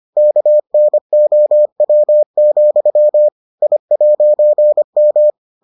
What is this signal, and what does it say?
Morse code, heard as a keyed tone, about 25 wpm, 600 Hz. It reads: KNOW, I'M